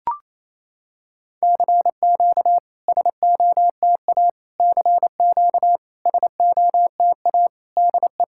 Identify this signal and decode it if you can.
Morse code, heard as a keyed tone, about 28 wpm, 700 Hz. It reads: CQ SOTA CQ SOTA DE
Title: SOTA QSO Practice 28wpm